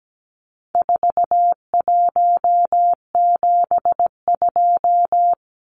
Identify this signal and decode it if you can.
Morse code, heard as a keyed tone, about 17 wpm, 700 Hz. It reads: 4172
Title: Numbers - 4-Digit 17wpm